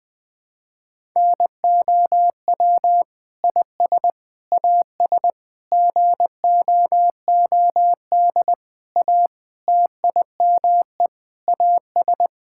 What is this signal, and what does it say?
Morse code, heard as a keyed tone, about 21 wpm, 700 Hz. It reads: NOW IS AS GOOD A TIME AS
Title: Sets of 8 Words - 20wpm